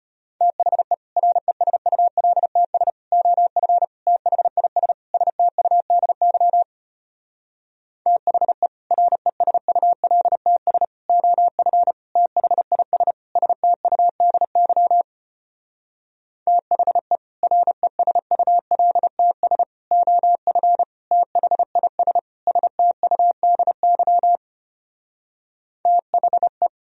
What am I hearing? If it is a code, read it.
Morse code, heard as a keyed tone, about 33 wpm, 700 Hz. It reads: THE RESULTS OF THIS STUDY THE RESULTS OF THIS STUDY THE RESULTS OF THIS STUDY THE